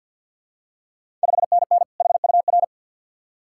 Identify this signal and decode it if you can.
Morse code, heard as a keyed tone, about 52 wpm, 690 Hz. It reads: 5NN HUR
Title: Ontario QSO Party Exchange 50wpm